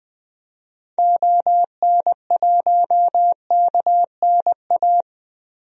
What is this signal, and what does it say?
Morse code, heard as a keyed tone, about 20 wpm, 700 Hz. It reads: ON1KNA
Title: DXCC-100 Call Signs - 6 Characters - Repeated 3x - 20wpm